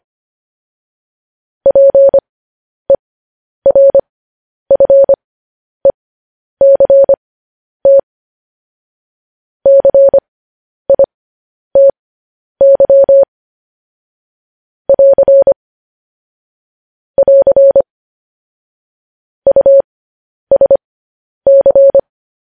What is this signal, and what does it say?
Morse code, heard as a keyed tone, about 25 wpm, spaced as for 10 wpm, 550 Hz. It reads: PERFECT CITY <AR> <AR> USC
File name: podcast / News headlines-1x-25x10